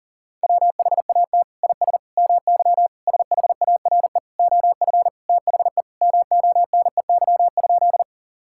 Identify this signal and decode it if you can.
Morse code, heard as a keyed tone, about 40 wpm, 700 Hz. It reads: WHAT IS MY SHARE OF THE MONEY?
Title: Sentences from Top 1000 Words - Review Part 6 - 40wpm